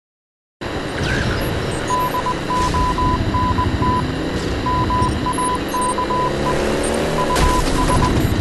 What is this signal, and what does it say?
Morse code, heard as a keyed tone, about 20 wpm, 1000 Hz. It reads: DONT MAKE US